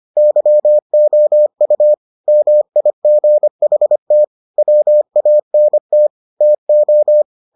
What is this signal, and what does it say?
Morse code, heard as a keyed tone, about 25 wpm, 600 Hz. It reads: YOU MIGHT WANT TO